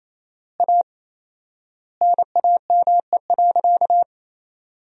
Morse code, heard as a keyed tone, about 28 wpm, 700 Hz. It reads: A NAME.
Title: 4x Word-Spacing - Sentences from Top 500 Words - 28wpm